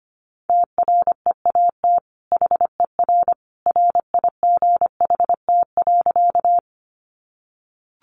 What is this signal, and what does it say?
Morse code, heard as a keyed tone, about 25 wpm, 700 Hz. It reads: TREAT HER RIGHT.
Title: Sentences from Top 1300 Words 25wpm